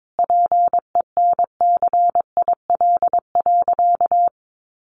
Morse code, heard as a keyed tone, about 22 wpm, 700 Hz. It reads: PENCIL.
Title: Additional Easy Phrases 22wpm